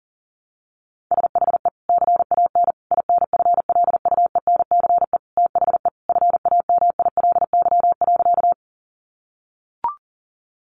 Morse code, heard as a keyed tone, about 40 wpm, 700 Hz. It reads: SHE CAN INFLUENCE THE FAMILY.
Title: Sentences from Top 1100 Words 40wpm